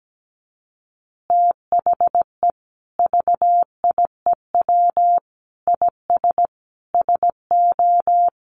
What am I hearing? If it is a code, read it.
Morse code, heard as a keyed tone, about 17 wpm, 700 Hz. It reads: THE VIEW IS SO
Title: Additional Easy Phrases 17wpm